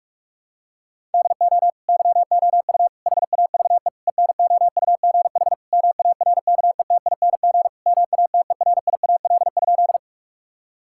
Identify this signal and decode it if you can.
Morse code, heard as a keyed tone, about 41 wpm, 700 Hz. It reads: DO YOU HAVE ENOUGH MARKETING MATERIAL?
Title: Sentences from Top 1000 Words - Review Part 5 - 45wpm